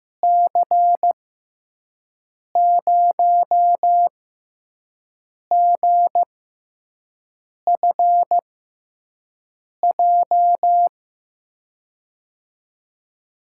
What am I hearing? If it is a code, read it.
Morse code, heard as a keyed tone, about 15 wpm, spaced as for 5 wpm, 700 Hz. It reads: C0GFJ
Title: USA Call Signs - 6 Characters - 6x Character Spacing - 15wpm